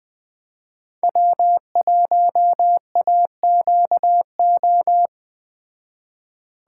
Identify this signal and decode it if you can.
Morse code, heard as a keyed tone, about 21 wpm, 700 Hz. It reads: W1AQO